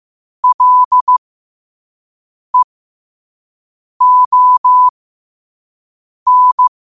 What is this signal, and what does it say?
Morse code, heard as a keyed tone, about 15 wpm, spaced as for 5 wpm, 1000 Hz. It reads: LEON
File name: qotdfiles / 5wpm